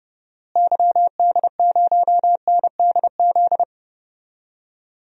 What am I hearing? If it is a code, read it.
Morse code, heard as a keyed tone, about 30 wpm, 700 Hz. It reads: YD0NDZ
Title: DXCC-100 Call Signs - 6 Characters - Repeated 3x - 30wpm